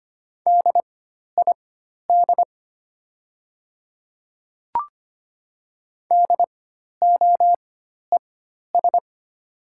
Morse code, heard as a keyed tone, about 25 wpm, spaced as for 12 wpm, 700 Hz. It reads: DID DOES
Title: 4x Character Spacing - Top 500 Words - 25wpm